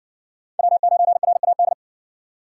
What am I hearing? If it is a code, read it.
Morse code, heard as a keyed tone, about 57 wpm, 700 Hz. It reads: W9RAD